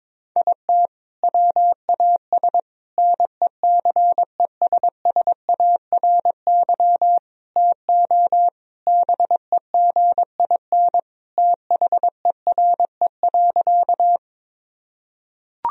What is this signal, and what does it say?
Morse code, heard as a keyed tone, about 22 wpm, 700 Hz. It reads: IT WAS NECESSARY TO BEGIN THERE.
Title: Sentences from Top 1000 Words - Review Part 1 - 22wpm